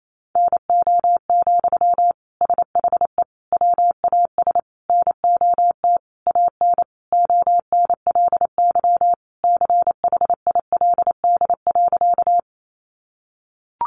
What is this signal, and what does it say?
Morse code, heard as a keyed tone, about 28 wpm, 700 Hz. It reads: NO, SHE WAS NOT AN ONLY CHILD.
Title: Sentences from Top 800 Words 28wpm